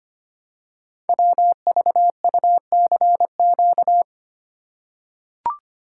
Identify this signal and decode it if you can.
Morse code, heard as a keyed tone, about 25 wpm, 700 Hz. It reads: WVUCQ
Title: Code Groups - 5 Letters - 25wpm